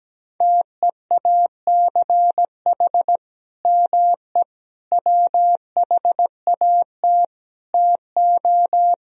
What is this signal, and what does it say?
Morse code, heard as a keyed tone, about 17 wpm, 700 Hz. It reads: TEACH ME WHAT TO
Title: Sentences from Top 500 Words and Abbreviations 17wpm